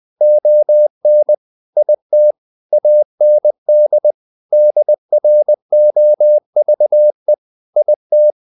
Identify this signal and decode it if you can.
Morse code, heard as a keyed tone, about 20 wpm, 600 Hz. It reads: ON IT AND DROVE IT
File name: the old man and the sea part 2 20wpm